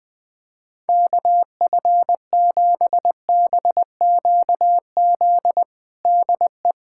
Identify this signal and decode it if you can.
Morse code, heard as a keyed tone, about 20 wpm, 700 Hz. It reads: KF7BQZ DE
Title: USA Callsign DE Callsign - 6 Characters - 20wpm